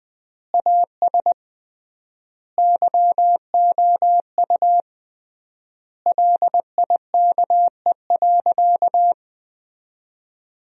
Morse code, heard as a keyed tone, about 20 wpm, 700 Hz. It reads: AS YOU LIKE.
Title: 3x Word-Spacing - Sentences from Top 500 Words - 20wpm